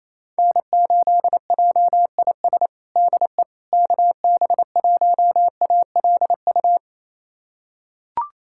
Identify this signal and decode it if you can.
Morse code, heard as a keyed tone, about 27 wpm, 700 Hz. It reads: N8JIS DE KB1ALU
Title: USA Callsign DE Callsign 28wpm